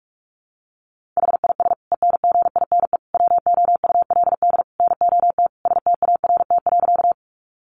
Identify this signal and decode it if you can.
Morse code, heard as a keyed tone, about 44 wpm, 700 Hz. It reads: HIS ENGINE WOULD NOT START.
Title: Sentences from Top 1500 Words 45wpm